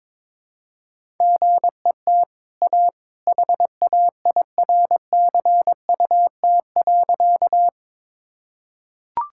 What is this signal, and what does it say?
Morse code, heard as a keyed tone, about 22 wpm, 690 Hz. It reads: GET A HAIRCUT.
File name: practice / 22wpm / Easy Sentences with Hints - 22wpm